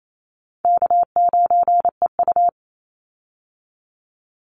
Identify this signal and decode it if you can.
Morse code, heard as a keyed tone, about 28 wpm, 700 Hz. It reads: K9EU
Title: Easy Call Signs - 28wpm